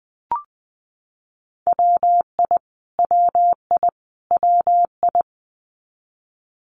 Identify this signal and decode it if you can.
Morse code, heard as a keyed tone, about 20 wpm, 700 Hz. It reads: WI WI WI
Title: US States Repeated 3x 20wpm